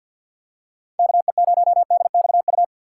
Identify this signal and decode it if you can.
Morse code, heard as a keyed tone, about 52 wpm, 700 Hz. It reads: KE0DXU
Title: USA Call Signs - 6 Characters - 3x Character Spacing - 50wpm